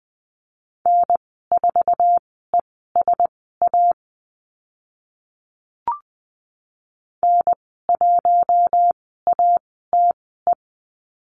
Morse code, heard as a keyed tone, about 20 wpm, spaced as for 15 wpm, 700 Hz. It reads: N4ESA N1ATE
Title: 082 Lesson 14g - Number 2 - Call Signs 20wpm